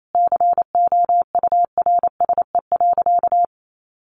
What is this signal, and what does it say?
Morse code, heard as a keyed tone, about 28 wpm, 700 Hz. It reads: COURSE.
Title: Sentences from Top 700 Words 28wpm